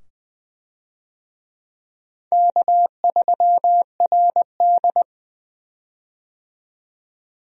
Morse code, heard as a keyed tone, about 20 wpm, 700 Hz. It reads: K3RD
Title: USA Call Signs - 4 Characters - 20wpm